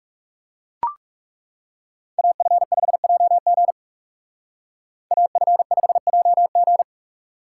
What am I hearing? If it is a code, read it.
Morse code, heard as a keyed tone, about 36 wpm, 700 Hz. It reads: AF5JG AF5JG
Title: Speed Racing - USA 2x2 Call Signs 30wpm